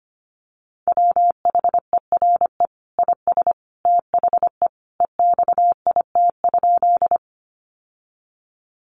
Morse code, heard as a keyed tone, about 25 wpm, 700 Hz. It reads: WHERE IS THE EXIT?
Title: Sentences from Top 1100 Words 25wpm